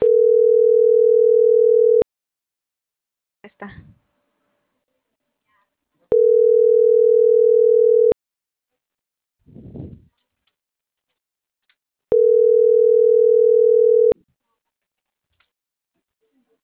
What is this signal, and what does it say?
Ringback tone (North America), 440 + 480 Hz.